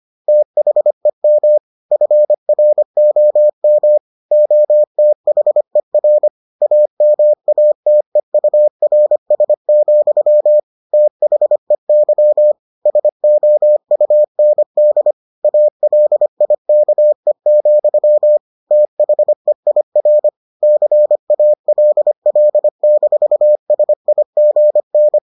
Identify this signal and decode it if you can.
Morse code, heard as a keyed tone, about 25 wpm, 600 Hz. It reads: THEM FROM OTHER AMATEURS, THEY SOUND ALIKE, THEIR CALL-SIGN